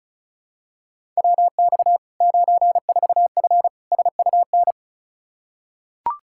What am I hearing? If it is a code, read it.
Morse code, heard as a keyed tone, about 34 wpm, 700 Hz. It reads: WX 94F SUN
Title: Practice QSO - Exchange 2 - Info - 35wpm